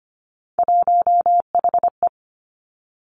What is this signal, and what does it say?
Morse code, heard as a keyed tone, about 25 wpm, 700 Hz. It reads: 1HE
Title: Three Challenging Letter-Number Combinations 25wpm